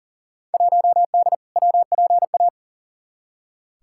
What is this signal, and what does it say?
Morse code, heard as a keyed tone, about 40 wpm, 700 Hz. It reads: 1D WPA
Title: Field Day Exchanges - 40wpm